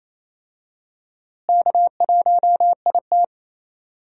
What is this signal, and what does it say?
Morse code, heard as a keyed tone, about 28 wpm, 700 Hz. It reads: K1IT
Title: USA Call Signs - 4 Characters - Repeated 2x - 28wpm